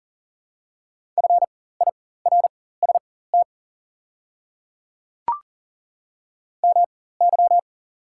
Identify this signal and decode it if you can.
Morse code, heard as a keyed tone, about 40 wpm, spaced as for 19 wpm, 690 Hz. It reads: FIRST MY
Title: 4x Character Spacing - Top 300 Words - 40wpm